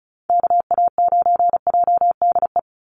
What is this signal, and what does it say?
Morse code, heard as a keyed tone, about 34 wpm, 700 Hz. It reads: KA9JDE
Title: USA Call Signs 35wpm